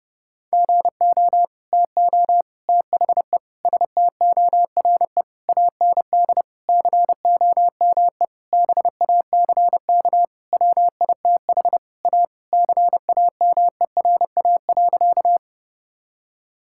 Morse code, heard as a keyed tone, about 32 wpm, 700 Hz. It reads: GO TO THE STORE AND COME BACK WITH A CAMERA.
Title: Sentences from Top 1000 Words - Review Part 5 - 30wpm